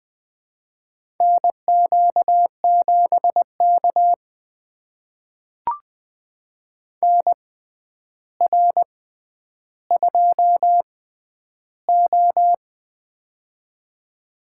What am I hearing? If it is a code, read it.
Morse code, heard as a keyed tone, about 20 wpm, spaced as for 10 wpm, 700 Hz. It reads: NQ7K NR2O
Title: USA Call Signs - 4 Characters - 6x Character Spacing - 20wpm